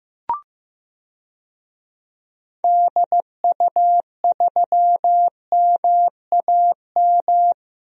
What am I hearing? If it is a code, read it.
Morse code, heard as a keyed tone, about 15 wpm, 700 Hz. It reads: DU3MAM
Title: DXCC-100 Call Signs - 6 Characters - 15wpm